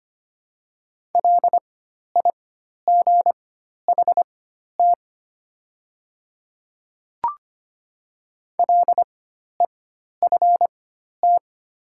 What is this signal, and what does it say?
Morse code, heard as a keyed tone, about 25 wpm, spaced as for 12 wpm, 700 Hz. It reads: LIGHT LEFT
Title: 4x Character Spacing - Top 300 Words - 25wpm